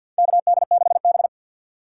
Morse code, heard as a keyed tone, about 53 wpm, 700 Hz. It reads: KD6B
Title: USA Call Signs - 6x Character Spacing - 50wpm